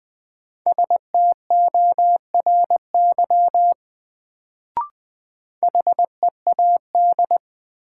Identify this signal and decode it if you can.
Morse code, heard as a keyed tone, about 21 wpm, 700 Hz. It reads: STORY HEAD
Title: Sentences from Top 300 Words - New Word Drill - 20wpm